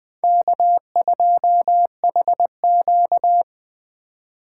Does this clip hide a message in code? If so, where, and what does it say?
Yes: Morse code from 0.2 to 3.4 s, heard as a keyed tone, about 20 wpm, 700 Hz. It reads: K2HQ